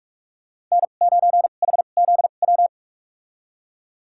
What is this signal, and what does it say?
Morse code, heard as a keyed tone, about 46 wpm, spaced as for 32 wpm, 700 Hz. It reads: N9HZW
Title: USA Call Signs - 5 Characters - 2x Character Spacing - 45wpm